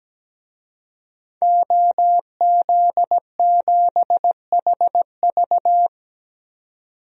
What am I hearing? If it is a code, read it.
Morse code, heard as a keyed tone, about 17 wpm, 700 Hz. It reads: OZ7HV
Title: DXCC-100 Call Signs - 5 Characters - 17wpm